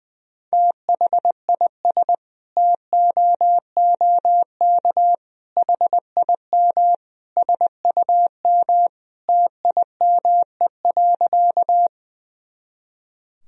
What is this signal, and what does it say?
Morse code, heard as a keyed tone, about 20 wpm, 700 Hz. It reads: THIS TOOK HIM SUM TIME.
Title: Sentences from Top 300 Words and Abbreviations 20wpm